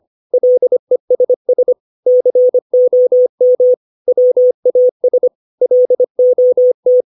Morse code, heard as a keyed tone, about 25 wpm, 500 Hz. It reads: LESS COM WAS LOT
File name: 632 words / 632 Words 25WPM